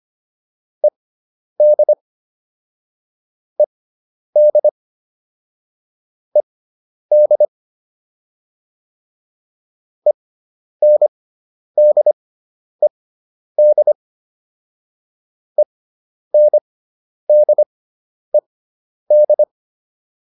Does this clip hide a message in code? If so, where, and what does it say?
Yes: Morse code from 0.8 to 19.4 s, heard as a keyed tone, about 25 wpm, spaced as for 10 wpm, 600 Hz. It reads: ED ED ED ENDED ENDED